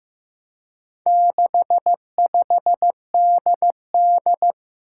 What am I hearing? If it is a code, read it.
Morse code, heard as a keyed tone, about 15 wpm, 700 Hz. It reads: 65DD
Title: Four Challenging Letter-Number Combinations 15wpm